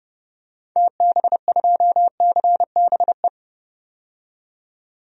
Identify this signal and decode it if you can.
Morse code, heard as a keyed tone, about 32 wpm, 700 Hz. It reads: TB2CBE
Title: DXCC-100 Call Signs - 6 Characters - 30wpm